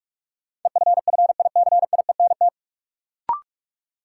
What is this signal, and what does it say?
Morse code, heard as a keyed tone, about 46 wpm, 700 Hz. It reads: EFFICIENT
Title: Sentences from Top 1300 Words - New Word Drill - 45wpm